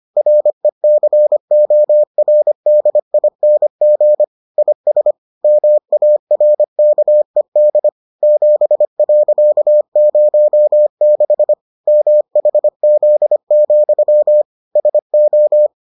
Morse code, heard as a keyed tone, about 25 wpm, 600 Hz. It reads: RECORDING IS MARKED 7.06 MHZ, SO